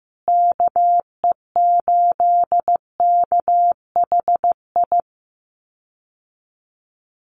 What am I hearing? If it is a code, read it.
Morse code, heard as a keyed tone, about 15 wpm, 700 Hz. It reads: KE8KHI
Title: USA Call Signs Repeated 2x 15wpm